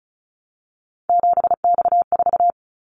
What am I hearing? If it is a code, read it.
Morse code, heard as a keyed tone, about 36 wpm, 700 Hz. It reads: 7X4
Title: Callsign Prefixes for Top 100 DXCC - 35wpm